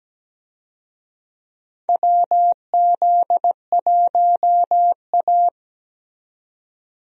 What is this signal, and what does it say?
Morse code, heard as a keyed tone, about 17 wpm, 700 Hz. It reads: WZ1A
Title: USA Call Signs - 4 Characters - 17wpm